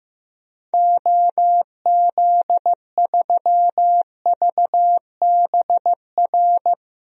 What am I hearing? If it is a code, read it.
Morse code, heard as a keyed tone, about 16 wpm, 700 Hz. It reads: OZ3VBR
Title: DXCC-100 Call Signs - 6 Characters - Repeated 2x - 15wpm